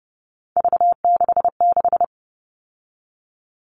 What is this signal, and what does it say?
Morse code, heard as a keyed tone, about 30 wpm, 700 Hz. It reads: V66
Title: All Cut Numbers - 3-Digit 30wpm